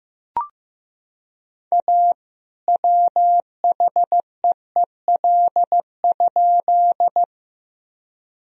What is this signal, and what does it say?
Morse code, heard as a keyed tone, about 15 wpm, 700 Hz. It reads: A WHEEL?
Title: Sentences from Top 1000 Words - Review Part 3 - 15wpm